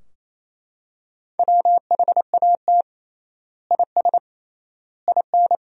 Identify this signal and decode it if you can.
Morse code, heard as a keyed tone, about 28 wpm, 700 Hz. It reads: WHAT IS IN
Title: 3x Word-Spacing - Sentences from Top 200 Words - 28wpm